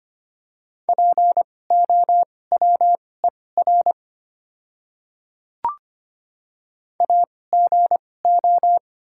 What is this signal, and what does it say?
Morse code, heard as a keyed tone, about 25 wpm, spaced as for 18 wpm, 690 Hz. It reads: POWER AGO
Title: 2x Character Spacing - Top 500 Words - 25wpm